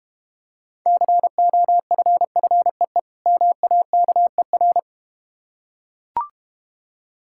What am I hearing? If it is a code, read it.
Morse code, heard as a keyed tone, about 32 wpm, 700 Hz. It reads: COFFEE MAKER